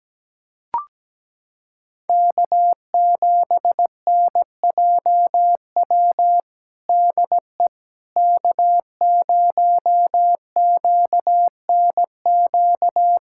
Morse code, heard as a keyed tone, about 17 wpm, 700 Hz. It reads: K7NJW DE K0QNQ
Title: USA Callsign DE Callsign - 5 Characters - 17wpm